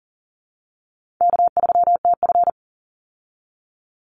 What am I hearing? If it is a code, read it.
Morse code, heard as a keyed tone, about 40 wpm, 700 Hz. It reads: K3TF
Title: Call Signs Repeated 2x - 40wpm